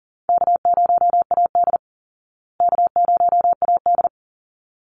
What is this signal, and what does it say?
Morse code, heard as a keyed tone, about 38 wpm, 700 Hz. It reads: K0AD K0AD
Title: Call Signs Repeated 3x - 40wpm